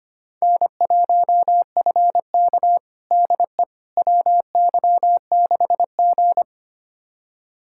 Morse code, heard as a keyed tone, about 25 wpm, 700 Hz. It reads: N1FK DE WY6G
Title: USA Callsign DE Callsign - 4 Characters - 25wpm